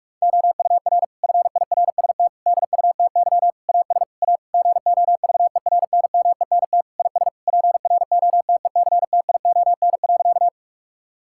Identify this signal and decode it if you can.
Morse code, heard as a keyed tone, about 46 wpm, 700 Hz. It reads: OUR FIRST DUTY AS A GOVERNMENT IS PROTECTION.
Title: Sentences from Top 1600 Words 45wpm